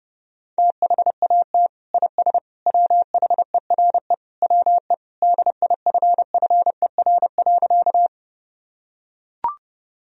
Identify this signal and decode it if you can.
Morse code, heard as a keyed tone, about 30 wpm, 700 Hz. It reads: THAT IS WHERE WE DIFFER.
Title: Sentences from Top 1000 Words - Review Part 1 - 30wpm